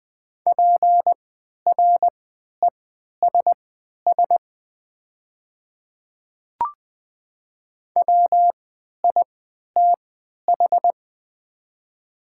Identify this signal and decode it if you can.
Morse code, heard as a keyed tone, about 20 wpm, spaced as for 11 wpm, 700 Hz. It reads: PRESS WITH
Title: 3x Character Spacing - Top 500 Words - 20wpm